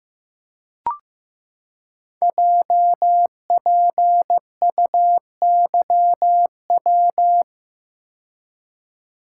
Morse code, heard as a keyed tone, about 15 wpm, 700 Hz. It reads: JPUYW
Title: Code Groups - 5 Letters - 15wpm